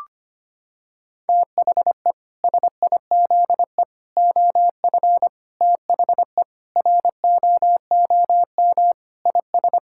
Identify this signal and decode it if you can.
Morse code, heard as a keyed tone, about 25 wpm, 700 Hz. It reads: THE SIZE OF THE ROOM IS
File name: practice / 25wpm / Sentences with 3-Word Phrases - 25wpm